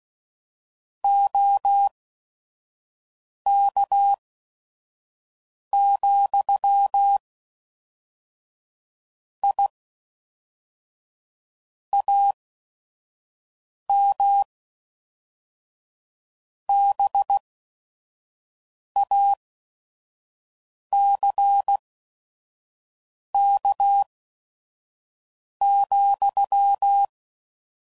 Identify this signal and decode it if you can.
Morse code, heard as a keyed tone, about 16 wpm, spaced as for 5 wpm, 800 Hz. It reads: OK, I AM BACK,
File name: CW Code Groups @5WPM / QSOs / kc7sac